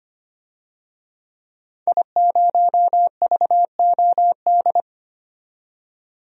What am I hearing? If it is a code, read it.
Morse code, heard as a keyed tone, about 25 wpm, 700 Hz. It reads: I0VOD
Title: DXCC-100 Call Signs - 5 Characters - 25wpm